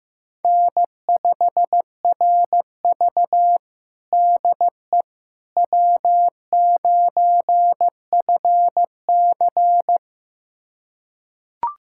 Morse code, heard as a keyed tone, about 15 wpm, 700 Hz. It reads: N5RV DE W9FC
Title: USA Callsign DE Callsign - 4 Characters - 15wpm